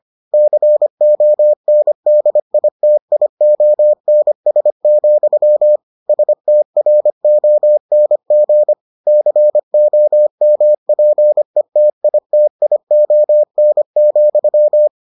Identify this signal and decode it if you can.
Morse code, heard as a keyed tone, about 25 wpm, 600 Hz. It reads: CONDITIONS, STRONG COMPETITION,